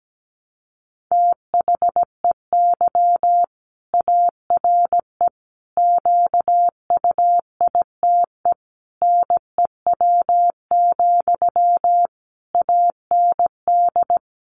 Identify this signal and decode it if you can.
Morse code, heard as a keyed tone, about 17 wpm, 700 Hz. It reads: THEY ARE QUITE NEW, AND
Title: Sentences from Top 700 Words 17wpm